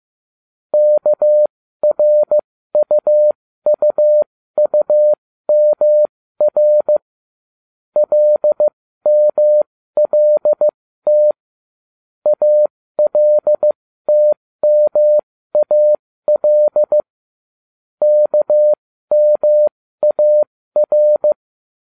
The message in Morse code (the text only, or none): KRUUUMR LMLT ALTMAL KMAR